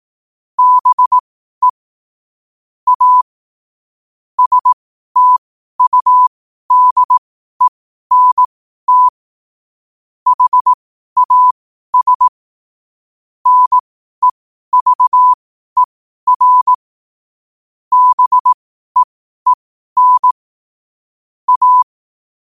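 Morse code, heard as a keyed tone, about 18 wpm, spaced as for 12 wpm, 1000 Hz. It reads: BE A STUDENT HAS NEVER BEEN A
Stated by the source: Morse code at 12 WPM.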